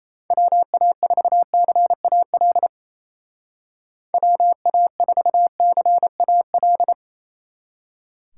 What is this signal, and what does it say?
Morse code, heard as a keyed tone, about 35 wpm, 700 Hz. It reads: WA4CAL WA4CAL
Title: Speed Racing - USA 2x3 Call Signs 28wpm